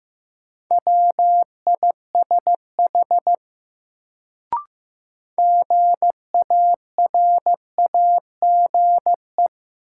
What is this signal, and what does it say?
Morse code, heard as a keyed tone, about 15 wpm, 700 Hz. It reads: WISH GARAGE
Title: Sentences from Top 1600 Words - New Word Drill Alt - 15wpm